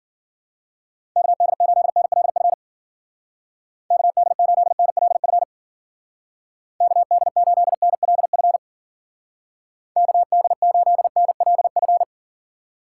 Morse code, heard as a keyed tone, about 48 wpm, 700 Hz. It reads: KD8NLF KD8NLF KD8NLF KD8NLF
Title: Speed Racing - USA 2x3 Call Signs 40wpm